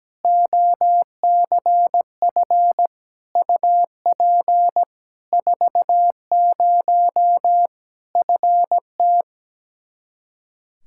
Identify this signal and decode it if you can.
Morse code, heard as a keyed tone, about 17 wpm, 700 Hz. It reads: OCF UP 40 FT
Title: Practice QSO - Exchange 2 - Info - 17wpm